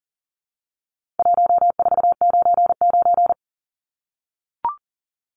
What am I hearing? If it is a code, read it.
Morse code, heard as a keyed tone, about 40 wpm, 700 Hz. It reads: 1499
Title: Numbers - 4-Digit 40wpm